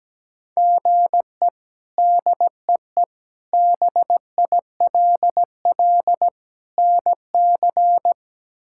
Morse code, heard as a keyed tone, about 17 wpm, 700 Hz. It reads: GE DEE BILL NC
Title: K1USN SST QSO Practice - 17wpm